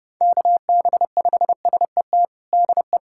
Morse code, heard as a keyed tone, about 30 wpm, 700 Hz. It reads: KB5SET DE
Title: USA Callsign DE Callsign - 6 Characters - 30wpm